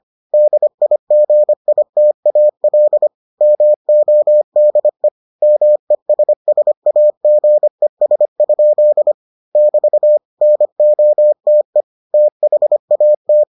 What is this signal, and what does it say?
Morse code, heard as a keyed tone, about 25 wpm, 600 Hz. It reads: DIGITAL MODE MESSAGES? <BT> NOTE THAT